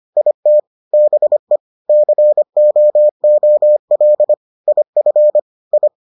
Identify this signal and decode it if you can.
Morse code, heard as a keyed tone, about 25 wpm, 600 Hz. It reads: IT BE COOL IF I